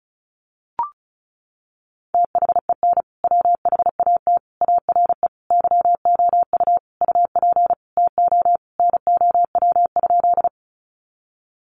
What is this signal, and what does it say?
Morse code, heard as a keyed tone, about 34 wpm, 700 Hz. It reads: THEN WHAT ARE YOU UP TO NOW?
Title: Sentences from Top 100 Words 35wpm